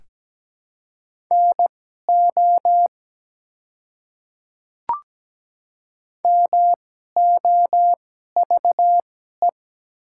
Morse code, heard as a keyed tone, about 17 wpm, spaced as for 12 wpm, 700 Hz. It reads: NO MOVE
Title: 2x Character Spacing - Top 200 Words - 17wpm